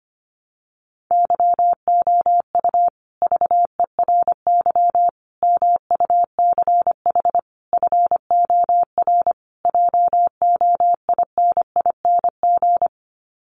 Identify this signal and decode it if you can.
Morse code, heard as a keyed tone, about 25 wpm, 700 Hz. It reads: YOU VERY MUCH FOR JOINING
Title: Sets of 5 Words Encore 25wpm